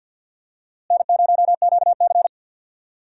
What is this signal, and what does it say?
Morse code, heard as a keyed tone, about 48 wpm, 700 Hz. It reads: N0QC
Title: USA Call Signs - 4 Characters - Repeated 3x - 50wpm